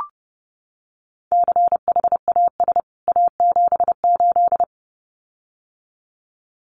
Morse code, heard as a keyed tone, about 30 wpm, 700 Hz. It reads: CHAS A78
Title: CWT Exchange 30wpm